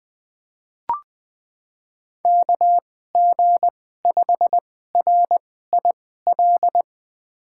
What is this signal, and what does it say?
Morse code, heard as a keyed tone, about 20 wpm, spaced as for 14 wpm, 700 Hz. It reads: KG5RIL
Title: USA Call Signs - 2x Character Spacing - 20wpm